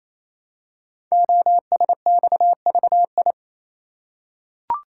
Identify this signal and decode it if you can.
Morse code, heard as a keyed tone, about 28 wpm, 700 Hz. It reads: OSXVI
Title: Code Groups - 5 Letters - 28wpm